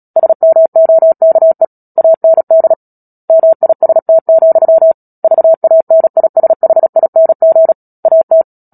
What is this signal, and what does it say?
Morse code, heard as a keyed tone, about 37 wpm, 650 Hz. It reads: SMOKE AND MIST, VANISHING AT
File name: Book Two / 10-Chapter 10 - Epilogue - 36 WPM